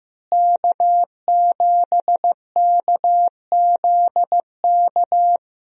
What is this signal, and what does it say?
Morse code, heard as a keyed tone, about 15 wpm, 700 Hz. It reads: K7KZK
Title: USA Call Signs - 5 Characters - Repeated 2x - 15wpm